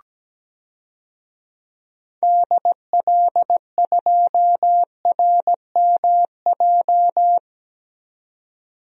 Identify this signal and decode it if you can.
Morse code, heard as a keyed tone, about 17 wpm, 700 Hz. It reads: DL2RMJ
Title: DXCC-100 Call Signs - 6 Characters - 17wpm